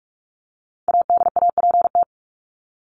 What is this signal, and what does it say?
Morse code, heard as a keyed tone, about 44 wpm, 700 Hz. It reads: ADAPT
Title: Top 1501-1750 Words 45wpm